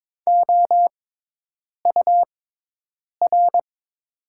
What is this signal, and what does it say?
Morse code, heard as a keyed tone, about 22 wpm, spaced as for 8 wpm, 700 Hz. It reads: OUR
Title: 6x Character Spacing - Top 300 Words - 22wpm